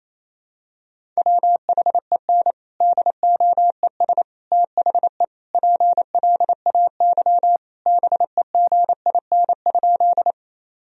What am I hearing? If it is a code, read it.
Morse code, heard as a keyed tone, about 28 wpm, 700 Hz. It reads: WHEN DOES THE PLAY BEGIN?
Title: Sentences from Top 1000 Words - Review Part 1 - 28wpm